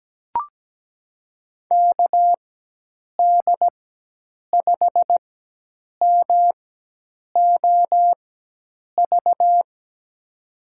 Morse code, heard as a keyed tone, about 17 wpm, spaced as for 8 wpm, 700 Hz. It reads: KD5MOV
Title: USA Call Signs - 4x Character Spacing - 17wpm